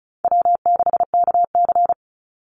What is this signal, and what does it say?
Morse code, heard as a keyed tone, about 36 wpm, 700 Hz. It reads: W6KC
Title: Call Signs - 35wpm